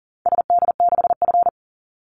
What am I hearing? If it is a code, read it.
Morse code, heard as a keyed tone, about 40 wpm, 700 Hz. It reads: SD6F
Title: Call Signs Repeated 2x - 40wpm